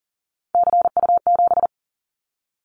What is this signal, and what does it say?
Morse code, heard as a keyed tone, about 40 wpm, 700 Hz. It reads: CU7
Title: Callsign Prefixes for Top 100 DXCC - 40wpm